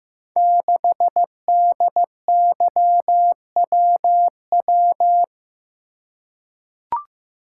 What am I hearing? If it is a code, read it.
Morse code, heard as a keyed tone, about 15 wpm, 700 Hz. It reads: 6DYWW
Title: Code Groups - 5 Letter:Numbers - 15wpm